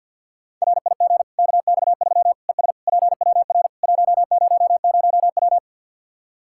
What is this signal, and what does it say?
Morse code, heard as a keyed tone, about 49 wpm, 700 Hz. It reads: RIG KX3 ES PWR 100W
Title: K1USN SST QSO Practice 50wpm